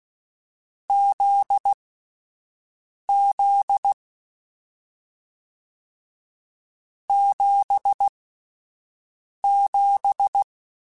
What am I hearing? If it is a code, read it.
Morse code, heard as a keyed tone, about 16 wpm, spaced as for 5 wpm, 800 Hz. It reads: ZZ 77
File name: Difficult Combos / Difficult Combos-mp3s / GZ GZ7 @5